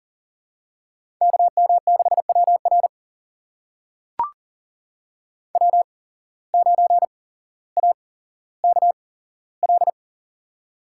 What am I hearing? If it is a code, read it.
Morse code, heard as a keyed tone, about 40 wpm, spaced as for 17 wpm, 700 Hz. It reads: KM6WR W9AKL
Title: USA Call Signs - 5 Characters - 8x Character Spacing - 40wpm